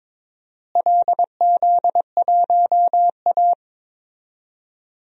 Morse code, heard as a keyed tone, about 22 wpm, 700 Hz. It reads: LZ1A
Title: DXCC-100 Call Signs - 4 Characters - Repeated 3x - 22wpm